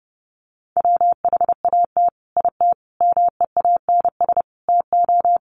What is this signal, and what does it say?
Morse code, heard as a keyed tone, about 30 wpm, 700 Hz. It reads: WHAT IT MEANS TO
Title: Sets of 5 Words 30wpm